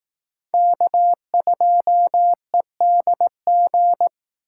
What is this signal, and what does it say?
Morse code, heard as a keyed tone, about 18 wpm, 700 Hz. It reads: K2EDG
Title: Speed Racing - USA 1x3 Call Signs 15wpm